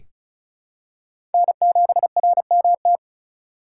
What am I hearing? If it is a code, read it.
Morse code, heard as a keyed tone, about 36 wpm, 700 Hz. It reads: N7RMT
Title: USA Call Signs - 5 Characters - 3x Character Spacing - 35wpm